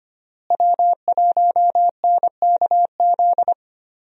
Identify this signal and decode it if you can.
Morse code, heard as a keyed tone, about 25 wpm, 700 Hz. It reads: W1NKZ
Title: USA Call Signs - 5 Characters - Repeated 3x - 25wpm